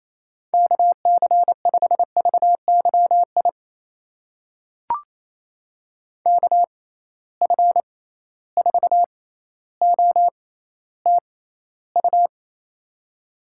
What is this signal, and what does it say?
Morse code, heard as a keyed tone, about 28 wpm, spaced as for 14 wpm, 700 Hz. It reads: KC5VYI KF4OTU
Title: USA Call Signs - 6 Characters - 6x Character Spacing - 28wpm